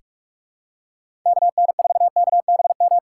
Morse code, heard as a keyed tone, about 44 wpm, 700 Hz. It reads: KN4KBM
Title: USA Call Signs - 6 Characters - 8x Character Spacing - 45wpm